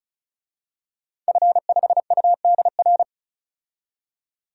Morse code, heard as a keyed tone, about 34 wpm, 700 Hz. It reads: F5UDR
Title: DXCC-100 Call Signs - 5 Characters - 35wpm